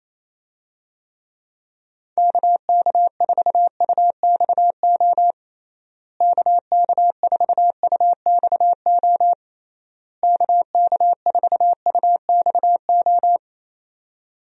Morse code, heard as a keyed tone, about 28 wpm, 700 Hz. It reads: KK4UXO KK4UXO KK4UXO
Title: USA Call Signs - 6 Characters - Repeated 3x - 28wpm